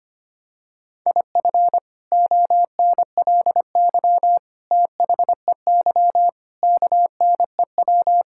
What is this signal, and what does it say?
Morse code, heard as a keyed tone, about 25 wpm, 700 Hz. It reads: IF ONLY THEY KNEW
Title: Sentences from Top 1000 Words - Review Part 6 - 25wpm